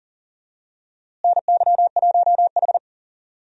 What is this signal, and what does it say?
Morse code, heard as a keyed tone, about 40 wpm, 700 Hz. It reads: NY1H
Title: USA Call Signs - 4 Characters - Repeated 2x - 40wpm